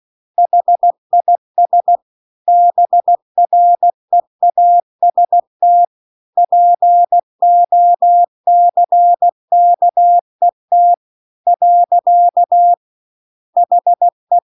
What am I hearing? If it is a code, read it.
Morse code, heard as a keyed tone, about 16 wpm, 700 Hz. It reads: HIS BREAST POCKET . HE
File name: CW-The-Big-Twoheaded-River-2-16WPM